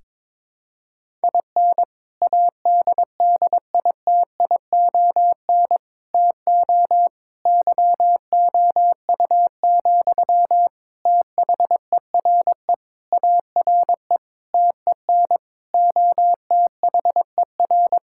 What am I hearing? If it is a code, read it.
Morse code, heard as a keyed tone, about 22 wpm, 700 Hz. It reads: IN ADDITION TO YOU, THERE ARE TEN OTHER
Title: Sentences from Top 1000 Words - Review Part 6 - 22wpm